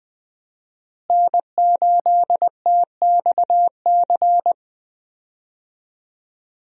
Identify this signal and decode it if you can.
Morse code, heard as a keyed tone, about 20 wpm, 700 Hz. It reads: N8TXC
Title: USA Call Signs - 5 Characters - Repeated 2x - 20wpm